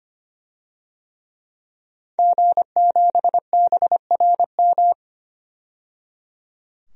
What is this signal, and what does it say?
Morse code, heard as a keyed tone, about 25 wpm, 700 Hz. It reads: G7BRM
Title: DXCC-100 Call Signs - 5 Characters - 25wpm